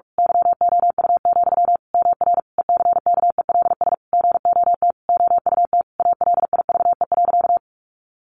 Morse code, heard as a keyed tone, about 44 wpm, 700 Hz. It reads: YOU, MR ECKELS GOT OUT ALIVE.
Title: A Sound Of Thunder 45wpm